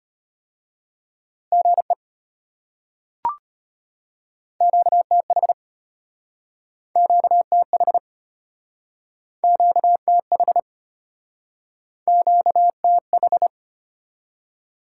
Morse code, heard as a keyed tone, about 34 wpm, 700 Hz. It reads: GE QTH QTH QTH QTH